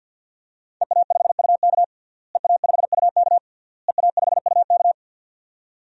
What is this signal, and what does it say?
Morse code, heard as a keyed tone, about 52 wpm, 700 Hz. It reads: EA5UK EA5UK EA5UK
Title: DXCC-100 Call Signs - 5 Characters - Repeated 3x - 50wpm